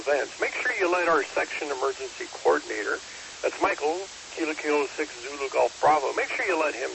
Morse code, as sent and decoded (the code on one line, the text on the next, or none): none